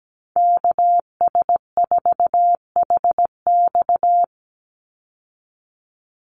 Morse code, heard as a keyed tone, about 17 wpm, 700 Hz. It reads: KS4HX
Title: USA Call Signs 17wpm